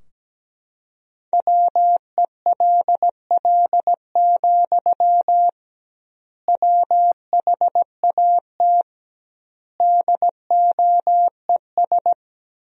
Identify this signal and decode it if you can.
Morse code, heard as a keyed tone, about 17 wpm, 700 Hz. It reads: WELL, WHAT DOES
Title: 2x Word-Spacing - Sentences from Top 500 Words - 17wpm